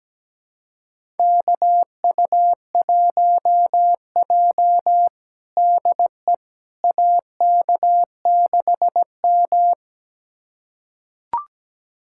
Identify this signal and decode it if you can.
Morse code, heard as a keyed tone, about 17 wpm, 700 Hz. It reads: KU1J DE AK6M
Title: USA Callsign DE Callsign - 4 Characters - 17wpm